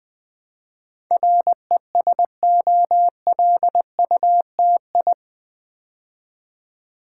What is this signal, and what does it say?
Morse code, heard as a keyed tone, about 20 wpm, 700 Hz. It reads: RESOLUTI
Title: Build A Word - 4 Letters At A Time - 10-Letter Words - 20wpm